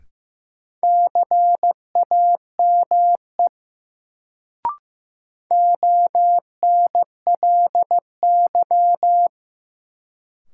Morse code, heard as a keyed tone, about 15 wpm, 700 Hz. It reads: CAME ONLY
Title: Sentences from Top 200 Words - New Word Drill - 15wpm